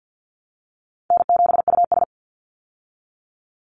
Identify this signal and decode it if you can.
Morse code, heard as a keyed tone, about 50 wpm, 700 Hz. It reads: N7US
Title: Call Signs Repeated 3x - 50wpm